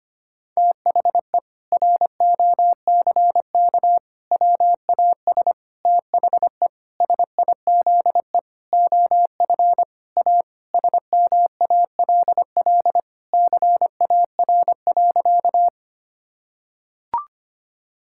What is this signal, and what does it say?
Morse code, heard as a keyed tone, about 25 wpm, 700 Hz. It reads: THE ROCK WAS THE SIZE OF A SMALL CAR.
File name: practice / 25wpm / Sentences with 4-Word Phrases - 25wpm